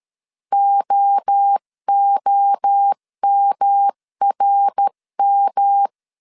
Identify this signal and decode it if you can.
Morse code, heard as a keyed tone, about 13 wpm, 800 Hz. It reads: OOMRM
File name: CW / DET / 1 3 1500C S A 12wpm 2905